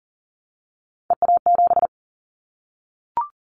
Morse code, heard as a keyed tone, about 38 wpm, 700 Hz. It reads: EA7
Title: Callsign Prefixes for Top 25 DXCC - Repeated 3x - 40wpm